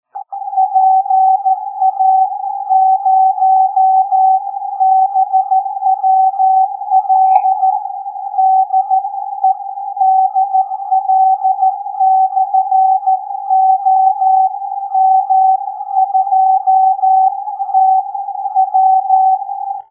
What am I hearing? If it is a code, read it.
Morse code, heard as a keyed tone, about 14 wpm, spaced as for 12 wpm, 760 Hz. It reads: PA0BWL DE DL/OM2TW